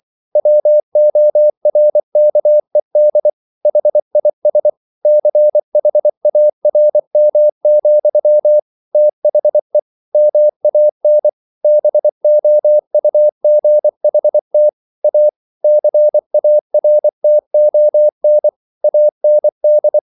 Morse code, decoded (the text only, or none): WORKED HIS CHARM, THE MAN BOUGHT A CARTON AND